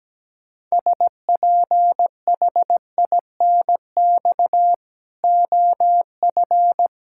Morse code, heard as a keyed tone, about 17 wpm, 700 Hz. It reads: SPHINX OF
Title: Pangram Challenge v2 - 17wpm